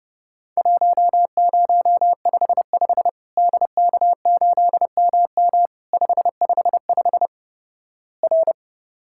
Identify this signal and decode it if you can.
Morse code, heard as a keyed tone, about 32 wpm, 700 Hz. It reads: 1055 DK8MM 555 R
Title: Worked All Europe DX Contest - QTC Exchange - 30wpm